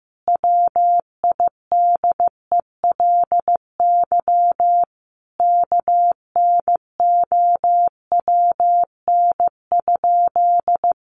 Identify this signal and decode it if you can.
Morse code, heard as a keyed tone, about 15 wpm, 700 Hz. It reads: WIDELY KNOWN?
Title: Sentences from Top 1200 Words 15wpm